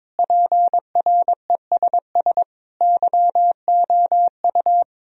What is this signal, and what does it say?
Morse code, heard as a keyed tone, about 22 wpm, 700 Hz. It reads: PRESS YOU
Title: Sentences from Top 1000 Words - Review Part 1 - 22wpm